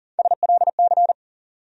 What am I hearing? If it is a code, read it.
Morse code, heard as a keyed tone, about 40 wpm, 700 Hz. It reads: SLC
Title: USA Call Sign Postfixes - 3 Letters 40wpm